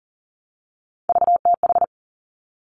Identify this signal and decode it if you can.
Morse code, heard as a keyed tone, about 42 wpm, 700 Hz. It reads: VTH
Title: Three Challenging Letter-Number Combinations 40wpm